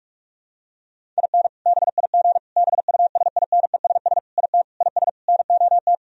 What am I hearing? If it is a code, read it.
Morse code, heard as a keyed tone, about 41 wpm, 700 Hz. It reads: IN BIG BUSINESS IT IS NOT
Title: Sentences from Top 1000 Words - Review Part 1 - 45wpm